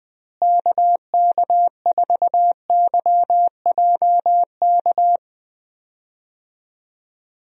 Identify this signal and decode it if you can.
Morse code, heard as a keyed tone, about 20 wpm, 700 Hz. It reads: KK4YJK
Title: USA Call Signs - 6 Characters - Repeated 2x - 20wpm